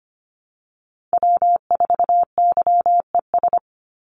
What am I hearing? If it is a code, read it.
Morse code, heard as a keyed tone, about 25 wpm, 700 Hz. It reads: W4YES
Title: Call Signs - 25wpm